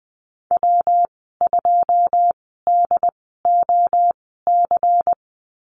Morse code, heard as a keyed tone, about 20 wpm, spaced as for 14 wpm, 700 Hz. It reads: W2DOC
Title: 161 Lesson 26g - Letter g - Call Signs 20wpm